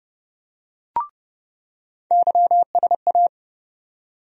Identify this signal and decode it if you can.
Morse code, heard as a keyed tone, about 30 wpm, 700 Hz. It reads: YSA
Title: USA Call Sign Postfixes - 3 Letters 30wpm